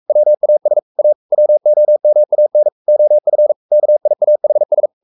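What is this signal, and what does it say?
Morse code, heard as a keyed tone, about 44 wpm, 600 Hz. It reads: WAS A WOMAN OF KIAHS
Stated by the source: Morse code at 43 wpm sending Ken Follett's novel Never in English, from CW med Gnister.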